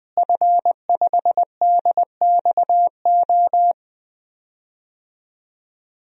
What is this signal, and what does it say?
Morse code, heard as a keyed tone, about 20 wpm, 700 Hz. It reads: F5DXO
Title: DXCC-100 Call Signs - 5 Characters - Repeated 3x - 20wpm